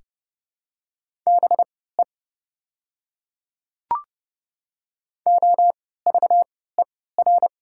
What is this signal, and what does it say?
Morse code, heard as a keyed tone, about 30 wpm, spaced as for 17 wpm, 700 Hz. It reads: BE OVER
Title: 3x Character Spacing - Top 100 Words - 30wpm